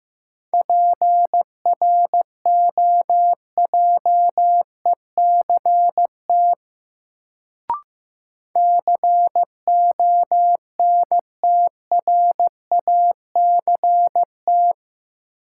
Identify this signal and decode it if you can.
Morse code, heard as a keyed tone, about 15 wpm, 700 Hz. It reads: PROJECT CONTRACT
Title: Sentences from Top 1400 Words - New Word Drill - 15wpm